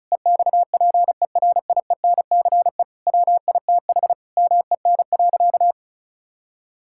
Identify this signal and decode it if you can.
Morse code, heard as a keyed tone, about 36 wpm, 700 Hz. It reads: EXPERIENCE WITH MEN.
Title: Sentences from Top 1000 Words - Review Part 5 - 35wpm